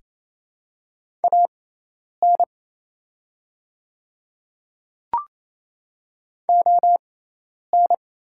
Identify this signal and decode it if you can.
Morse code, heard as a keyed tone, about 28 wpm, spaced as for 10 wpm, 700 Hz. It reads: AN ON
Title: 6x Character Spacing - Top 200 Words - 28wpm